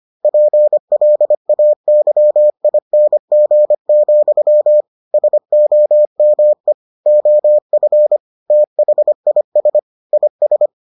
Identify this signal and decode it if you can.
Morse code, heard as a keyed tone, about 25 wpm, 600 Hz. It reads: PLAYING, SOME OF THIS IS